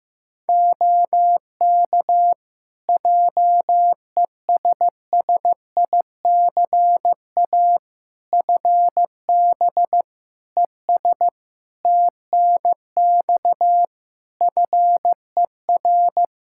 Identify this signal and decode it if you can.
Morse code, heard as a keyed tone, about 15 wpm, 700 Hz. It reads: OK JESSICA FB ES TNX FER
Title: Practice QSO - Exchange 2 - Info - 15wpm